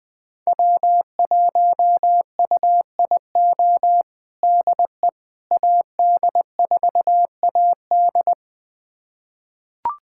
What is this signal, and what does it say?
Morse code, heard as a keyed tone, about 20 wpm, 700 Hz. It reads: W1UIO DE AD4AD
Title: USA Callsign DE Callsign - 5 Characters - 20wpm